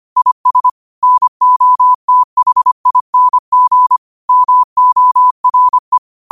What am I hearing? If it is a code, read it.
Morse code, heard as a keyed tone, about 25 wpm, 1000 Hz. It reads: IS NOTHING MORE